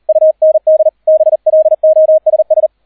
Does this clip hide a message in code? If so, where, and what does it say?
Yes: Morse code from 0.1 to 2.7 s, heard as a keyed tone, about 38 wpm, 630 Hz. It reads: UND BLOSS